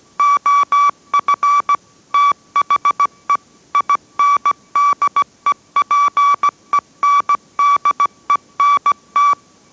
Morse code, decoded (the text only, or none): OF THE INDEPENDENT